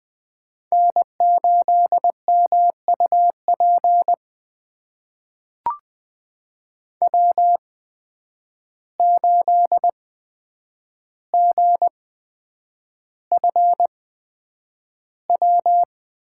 Morse code, heard as a keyed tone, about 20 wpm, spaced as for 9 wpm, 700 Hz. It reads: N8MUP W8GFW
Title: USA Call Signs - 5 Characters - 8x Character Spacing - 20wpm